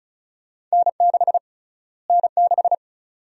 Morse code, heard as a keyed tone, about 36 wpm, 700 Hz. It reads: N6 N6
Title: USA Call Sign Prefixes - Repeated 2x - 35wpm